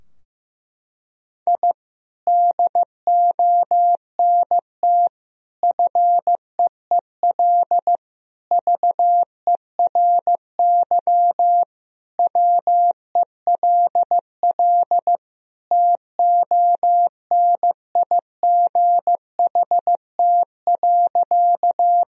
Morse code, heard as a keyed tone, about 14 wpm, 700 Hz. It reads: I DONT FEEL VERY WELL TONIGHT.
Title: Sentences from Top 1000 Words - Review Part 3 - 15wpm